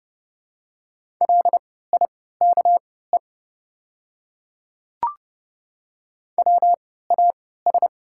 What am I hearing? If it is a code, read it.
Morse code, heard as a keyed tone, about 30 wpm, spaced as for 17 wpm, 700 Hz. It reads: LIKE WAS
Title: 3x Character Spacing - Top 100 Words - 30wpm